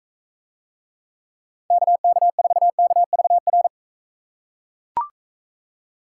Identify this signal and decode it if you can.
Morse code, heard as a keyed tone, about 44 wpm, 700 Hz. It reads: KK4KVR